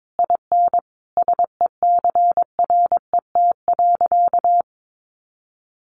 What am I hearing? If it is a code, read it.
Morse code, heard as a keyed tone, about 22 wpm, 700 Hz. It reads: IN SECRET.
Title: Sentences from Top 1200 Words 22wpm